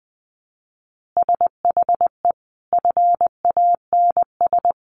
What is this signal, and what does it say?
Morse code, heard as a keyed tone, about 19 wpm, 700 Hz. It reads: SHE FANS
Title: Easy Phrases 20wpm